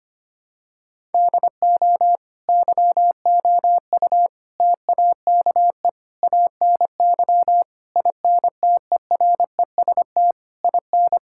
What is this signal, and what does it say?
Morse code, heard as a keyed tone, about 25 wpm, 700 Hz. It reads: DO YOU TAKE ANY INTEREST IN
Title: Sentences from Top 1000 Words - Review Part 6 - 25wpm